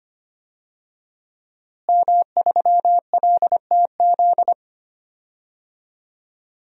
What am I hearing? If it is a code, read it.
Morse code, heard as a keyed tone, about 25 wpm, 700 Hz. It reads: M3LTZ
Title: DXCC-100 Call Signs - 5 Characters - Repeated 3x - 25wpm